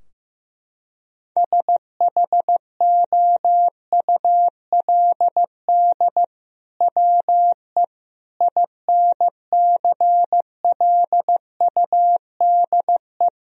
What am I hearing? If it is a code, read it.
Morse code, heard as a keyed tone, about 15 wpm, 700 Hz. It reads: SHOULD WE INCLUDE
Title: Sentences from Top 1000 Words - Review Part 3 - 15wpm